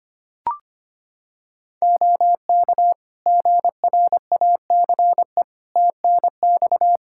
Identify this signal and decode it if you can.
Morse code, heard as a keyed tone, about 25 wpm, 700 Hz. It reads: OK GRACE TNX
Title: Practice QSO - Exchange 4 - Ending - 25wpm